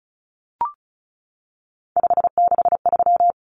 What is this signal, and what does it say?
Morse code, heard as a keyed tone, about 34 wpm, 700 Hz. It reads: 563
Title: Common Cut Numbers - 3-Digit 35wpm